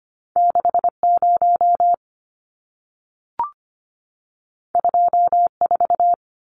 Morse code, heard as a keyed tone, about 25 wpm, 700 Hz. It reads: 60 24
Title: Numbers - 2-Digit 25wpm